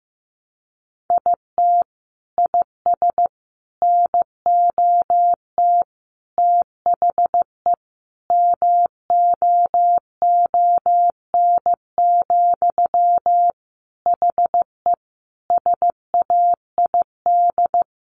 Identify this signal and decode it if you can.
Morse code, heard as a keyed tone, about 15 wpm, 700 Hz. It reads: IT IS NOT THE MOON, HE SAID
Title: Sentences from Top 1100 Words 15wpm